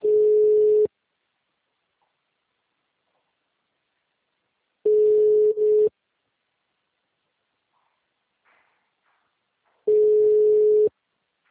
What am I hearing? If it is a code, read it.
Ringback tone (Europe, 425 Hz).